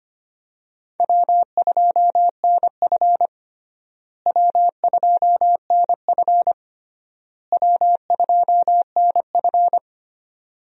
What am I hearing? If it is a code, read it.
Morse code, heard as a keyed tone, about 25 wpm, 700 Hz. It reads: W2NF W2NF W2NF
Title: USA Call Signs - 4 Characters - Repeated 3x - 25wpm